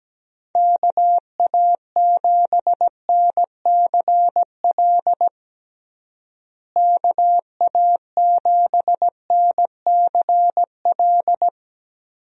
Morse code, heard as a keyed tone, about 17 wpm, 700 Hz. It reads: KA7NCL KA7NCL
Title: USA Call Signs - 6 Characters - Repeated 3x - 17wpm